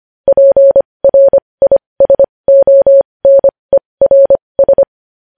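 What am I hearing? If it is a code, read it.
Morse code, heard as a keyed tone, about 25 wpm, spaced as for 20 wpm, 550 Hz. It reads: PRISONERS
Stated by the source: Morse news headlines sent at 20.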